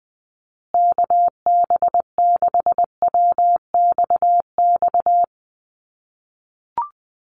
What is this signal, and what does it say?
Morse code, heard as a keyed tone, about 20 wpm, 700 Hz. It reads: KB6WXX
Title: USA Call Signs Repeated 2x 20wpm